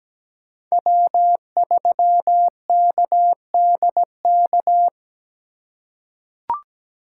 Morse code, heard as a keyed tone, about 17 wpm, 700 Hz. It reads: W3KDK
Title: USA Call Signs - 5 Characters - 2x Character Spacing - 17wpm